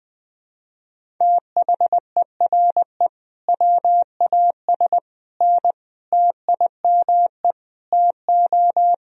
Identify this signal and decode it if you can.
Morse code, heard as a keyed tone, about 21 wpm, 700 Hz. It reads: THERE WAS N TIME TO
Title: Sentences from Top 400 Words and Abbreviations 20wpm